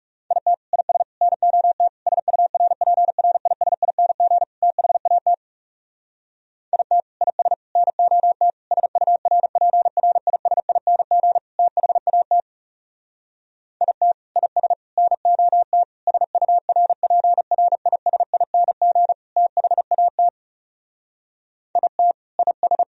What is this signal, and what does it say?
Morse code, heard as a keyed tone, about 37 wpm, 700 Hz. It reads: IT IS NOT SURPRISING THAT IT IS NOT SURPRISING THAT IT IS NOT SURPRISING THAT IT IS
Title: Speed-Racing - Sets of 5 Words 30wpm